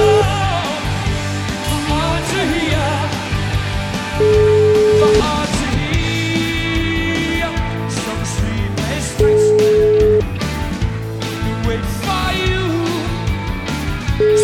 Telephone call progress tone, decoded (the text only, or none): ringback tone